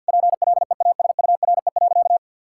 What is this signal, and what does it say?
Morse code, heard as a keyed tone, about 50 wpm, 700 Hz. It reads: PLEASURE.
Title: Sentences from Top 1600 Words 50wpm